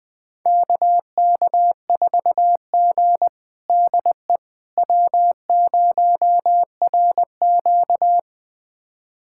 Morse code, heard as a keyed tone, about 20 wpm, 700 Hz. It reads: KK4G DE W0RQ
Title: USA Callsign DE Callsign - 4 Characters - 20wpm